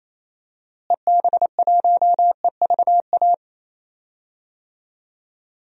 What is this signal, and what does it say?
Morse code, heard as a keyed tone, about 28 wpm, 700 Hz. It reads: EB1EVA
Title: DXCC-100 Call Signs - 6 Characters - Repeated 2x - 28wpm